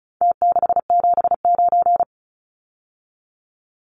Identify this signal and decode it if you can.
Morse code, heard as a keyed tone, about 34 wpm, 700 Hz. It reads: T679
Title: Numbers - Contest Style - 4-Digit 35wpm